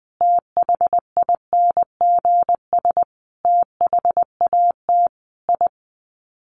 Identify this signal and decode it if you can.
Morse code, heard as a keyed tone, about 20 wpm, 700 Hz. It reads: THINGS THAT I
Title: Sets of 5 Words Encore 20wpm